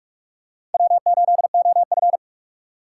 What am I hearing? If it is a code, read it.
Morse code, heard as a keyed tone, about 46 wpm, 700 Hz. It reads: W8OF
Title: USA Call Signs - 4 Characters - 4x Character Spacing - 45wpm